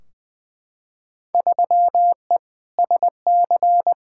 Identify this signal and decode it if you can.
Morse code, heard as a keyed tone, about 20 wpm, 700 Hz. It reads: 3E SC
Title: Field Day Exchanges Encore - 20wpm